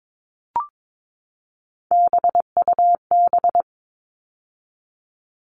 Morse code, heard as a keyed tone, about 22 wpm, 700 Hz. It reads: BUB
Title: All Cut Numbers - 3-Digit 22wpm